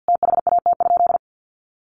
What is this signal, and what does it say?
Morse code, heard as a keyed tone, about 50 wpm, 700 Hz. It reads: THAT?
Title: Sentences from Top 1100 Words 50wpm